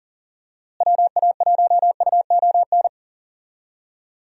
Morse code, heard as a keyed tone, about 40 wpm, 700 Hz. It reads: WA1UON
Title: USA Call Signs - 6 Characters - Repeated 2x - 40wpm